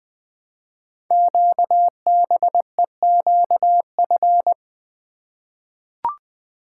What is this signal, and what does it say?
Morse code, heard as a keyed tone, about 20 wpm, 700 Hz. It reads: QBEQF
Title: Code Groups - 5 Letters - 20wpm